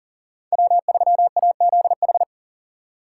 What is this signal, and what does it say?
Morse code, heard as a keyed tone, about 40 wpm, 700 Hz. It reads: W3AZH
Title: USA Call Signs - 5 Characters - 3x Character Spacing - 40wpm